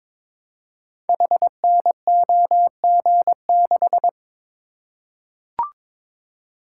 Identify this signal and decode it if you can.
Morse code, heard as a keyed tone, about 22 wpm, 700 Hz. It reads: HNOG6
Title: Code Groups - 5 Letter:Numbers - 22wpm